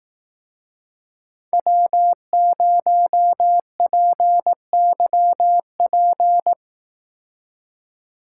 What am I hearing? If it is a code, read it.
Morse code, heard as a keyed tone, about 18 wpm, 700 Hz. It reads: W0PYP
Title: Speed Racing - USA 1x3 Call Signs 15wpm